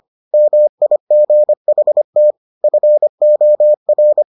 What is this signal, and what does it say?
Morse code, heard as a keyed tone, about 25 wpm, 600 Hz. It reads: MIGHT FOR